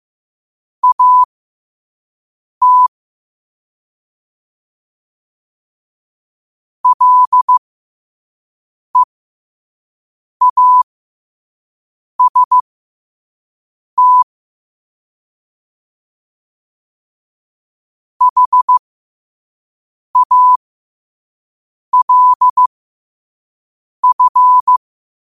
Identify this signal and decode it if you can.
Morse code, heard as a keyed tone, about 15 wpm, spaced as for 5 wpm, 1000 Hz. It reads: AT LEAST HALF
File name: qotdfiles / 5wpm